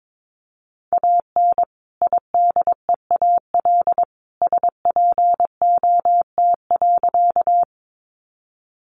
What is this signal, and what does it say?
Morse code, heard as a keyed tone, about 22 wpm, 700 Hz. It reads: AN IDEAL SPOT.
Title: Sentences from Top 1400 Words 22wpm